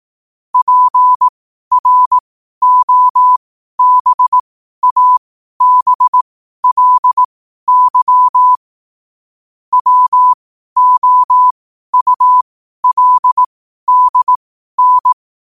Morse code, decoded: PROBABLY WOULDN

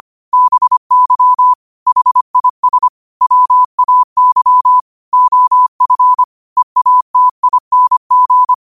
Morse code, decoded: BY HIS WAY OF EATING

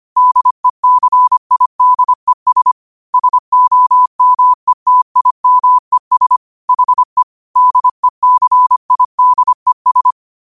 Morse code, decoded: DECIDES SOMETIMES HE DECIDES

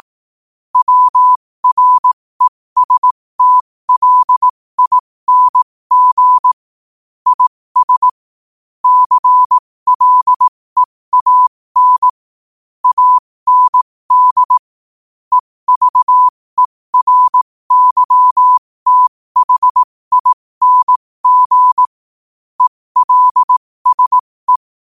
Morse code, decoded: WRESTLING IS CLEAN AND EVERYTHING ELSE

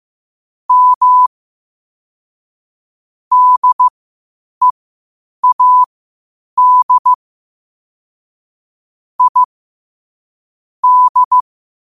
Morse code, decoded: M DEAD I D

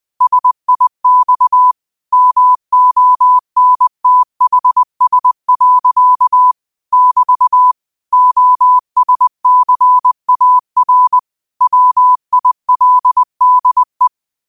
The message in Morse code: SIX MONTHS. <BT> OSCAR WILDE

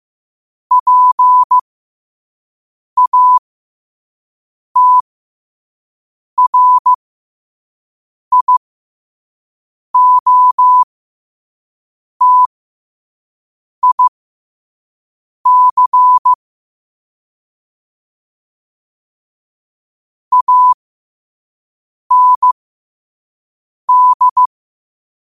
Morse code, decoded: PATRIOTIC AND

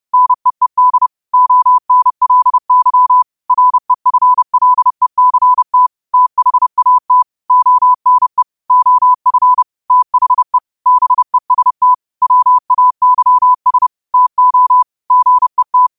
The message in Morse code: NEED ONLY REFLECT THAT ONE OF THE BEST WAYS TO GET

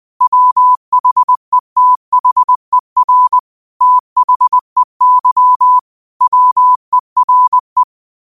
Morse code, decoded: WHETHER THEY WERE

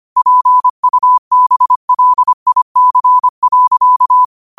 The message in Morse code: PUBLIC.

